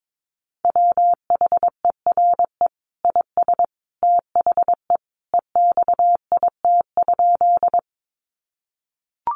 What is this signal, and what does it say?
Morse code, heard as a keyed tone, about 22 wpm, 700 Hz. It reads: WHERE IS THE EXIT?
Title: Sentences from Top 1100 Words 22wpm